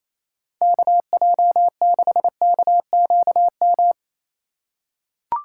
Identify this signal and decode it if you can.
Morse code, heard as a keyed tone, about 27 wpm, 700 Hz. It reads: KJ6KQM
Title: USA Call Signs - 6 Characters - 2x Character Spacing - 28wpm